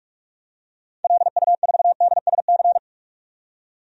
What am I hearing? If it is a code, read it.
Morse code, heard as a keyed tone, about 41 wpm, 700 Hz. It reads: LU4DSC
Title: DXCC-100 Call Signs - 6 Characters - 45wpm